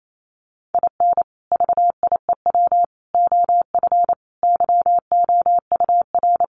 Morse code, heard as a keyed tone, about 28 wpm, 700 Hz. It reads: IN VIEW OF YOUR
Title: Sentences from Top 900 Words 28wpm